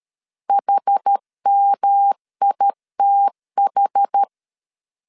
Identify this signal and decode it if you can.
Morse code, heard as a keyed tone, about 13 wpm, 800 Hz. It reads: HMITH
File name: CW / DET / 1 3 1500C S B 12wpm 2828